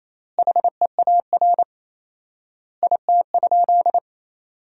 Morse code, heard as a keyed tone, about 28 wpm, 700 Hz. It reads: HEAR IT?
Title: 4x Word-Spacing - Sentences from Top 300 Words - 28wpm